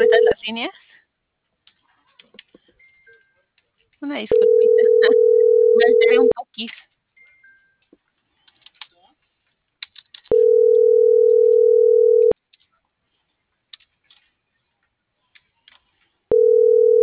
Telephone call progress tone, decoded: ringback tone

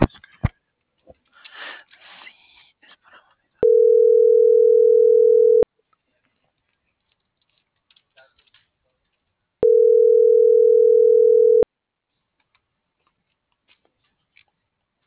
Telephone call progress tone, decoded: ringback tone

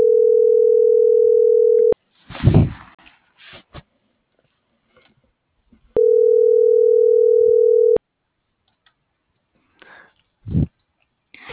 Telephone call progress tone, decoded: ringback tone

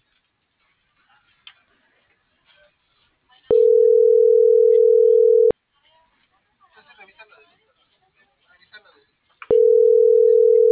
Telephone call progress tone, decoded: ringback tone

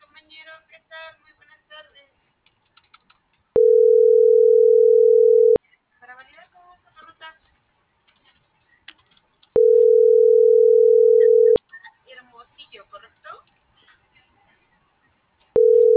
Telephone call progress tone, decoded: ringback tone

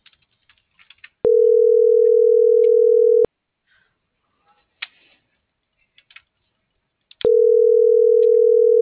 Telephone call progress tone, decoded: ringback tone

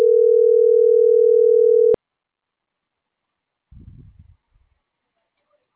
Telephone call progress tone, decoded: ringback tone

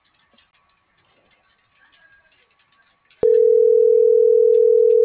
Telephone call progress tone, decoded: ringback tone

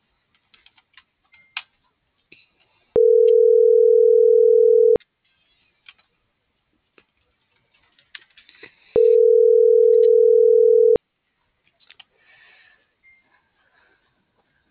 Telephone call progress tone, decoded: ringback tone

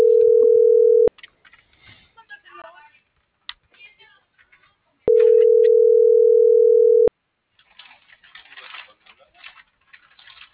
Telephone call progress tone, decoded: ringback tone

